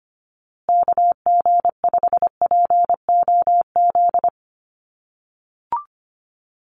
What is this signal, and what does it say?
Morse code, heard as a keyed tone, about 25 wpm, 700 Hz. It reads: KG5POZ
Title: USA Call Signs Repeated 2x 25wpm